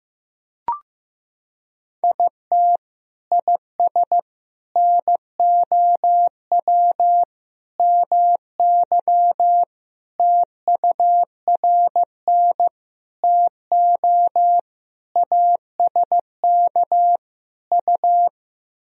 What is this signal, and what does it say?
Morse code, heard as a keyed tone, about 15 wpm, 700 Hz. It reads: IT IS NOW MY TURN TO ASK U
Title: Sentences from Top 200 Words and Abbreviations 15wpm